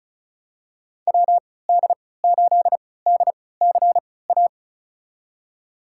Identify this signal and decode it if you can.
Morse code, heard as a keyed tone, about 36 wpm, spaced as for 20 wpm, 700 Hz. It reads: WD8DCA
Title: USA Call Signs - 6 Characters - 3x Character Spacing - 35wpm